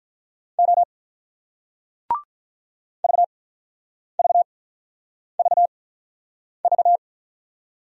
Morse code, heard as a keyed tone, about 47 wpm, spaced as for 9 wpm, 700 Hz. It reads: O VVVV